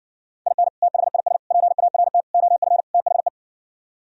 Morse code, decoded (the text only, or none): IN THIS PART OF THE